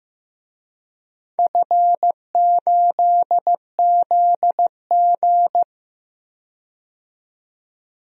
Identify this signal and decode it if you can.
Morse code, heard as a keyed tone, about 15 wpm, 700 Hz. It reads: F8ZG
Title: DXCC-100 Call Signs - 4 Characters - 15wpm